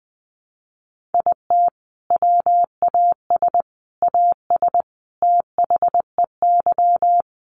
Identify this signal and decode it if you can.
Morse code, heard as a keyed tone, about 20 wpm, 700 Hz. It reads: IT WAS AS THEY
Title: Sentences from Top 1500 Words 20wpm